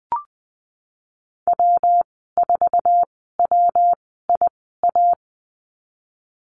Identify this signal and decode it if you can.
Morse code, heard as a keyed tone, about 20 wpm, spaced as for 15 wpm, 700 Hz. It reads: W4WIA
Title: 147 Lesson 24g - Letter y - Call Signs 20wpm